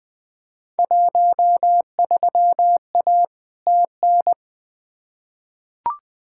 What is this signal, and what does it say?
Morse code, heard as a keyed tone, about 20 wpm, 700 Hz. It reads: 13A TN
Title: Field Day Exchanges Encore - 20wpm